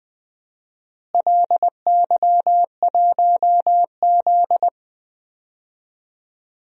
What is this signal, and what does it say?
Morse code, heard as a keyed tone, about 20 wpm, 700 Hz. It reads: LY1Z
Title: DXCC-100 Call Signs - 4 Characters - Repeated 3x - 20wpm